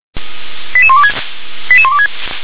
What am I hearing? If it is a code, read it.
Five-tone selective call (ZVEI), heard as quick runs of five tones, tones of about 70 ms. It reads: 88126 88126